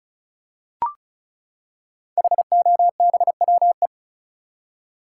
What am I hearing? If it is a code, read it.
Morse code, heard as a keyed tone, about 36 wpm, 700 Hz. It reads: HOBWE
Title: Code Groups - 5 Letters - 35wpm